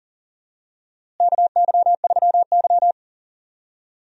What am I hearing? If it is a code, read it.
Morse code, heard as a keyed tone, about 37 wpm, 700 Hz. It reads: KY3Y
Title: USA Call Signs - 4 Characters - Repeated 2x - 40wpm